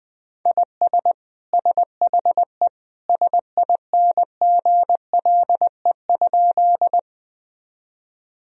Morse code, heard as a keyed tone, about 20 wpm, 700 Hz. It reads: IS SHE SINGLE?
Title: Sentences from Top 1000 Words - Review Part 4 - 20wpm